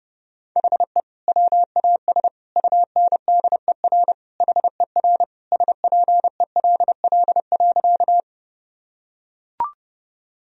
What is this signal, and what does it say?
Morse code, heard as a keyed tone, about 30 wpm, 700 Hz. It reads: HE WAS UNDER HER SPELL.
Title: Sentences from Top 1000 Words - Review Part 1 - 30wpm